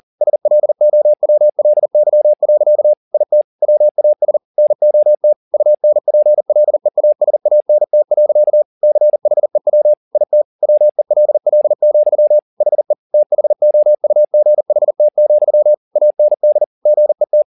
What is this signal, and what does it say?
Morse code, heard as a keyed tone, about 40 wpm, 600 Hz. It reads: SLOWLY. IT WAS NOT UNPLEASANT. CHEW IT WELL, HE THOUGHT, AND GET